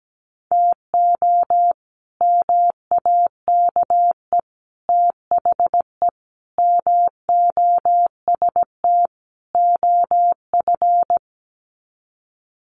Morse code, heard as a keyed tone, about 17 wpm, 700 Hz. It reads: TO MAKE THE MOST OF
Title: Sets of 5 Words Encore2 17wpm